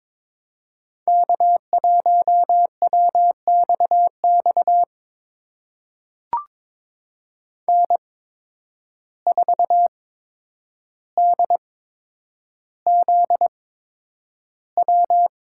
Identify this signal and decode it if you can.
Morse code, heard as a keyed tone, about 23 wpm, spaced as for 10 wpm, 700 Hz. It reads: K1WXX N4DZW
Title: USA Call Signs - 5 Characters - 8x Character Spacing - 22wpm